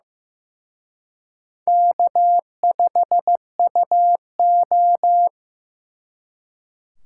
Morse code, heard as a keyed tone, about 18 wpm, 700 Hz. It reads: K5UO
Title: Speed Racing - USA 1x2 Call Signs 15wpm